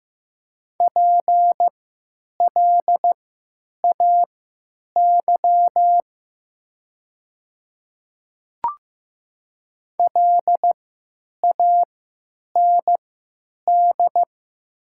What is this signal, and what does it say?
Morse code, heard as a keyed tone, about 15 wpm, spaced as for 9 wpm, 700 Hz. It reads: PLAY LAND
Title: 3x Character Spacing - Top 300 Words - 15wpm